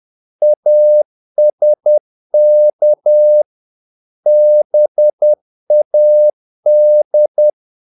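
Morse code, heard as a keyed tone, about 10 wpm, 600 Hz. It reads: ASK BAD
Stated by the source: Morse practice of short Danish words at 10 wpm, from CW med Gnister.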